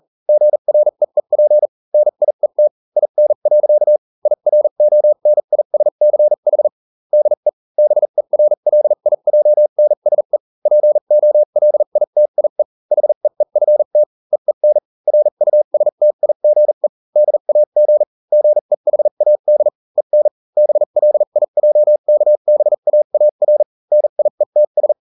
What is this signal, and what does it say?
Morse code, heard as a keyed tone, about 40 wpm, spaced as for 35 wpm, 600 Hz. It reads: GREEP NIET IN. IRONISCH DE BERLIJNSE POLITIE HEEFT EEN RUSTIGE DAG GEHAD EN BLIJKBAAR NIETS